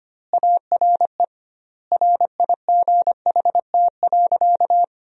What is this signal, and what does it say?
Morse code, heard as a keyed tone, about 25 wpm, 700 Hz. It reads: ARE RIGHT.
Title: 2x Word-Spacing - Sentences from Top 500 Words - 25wpm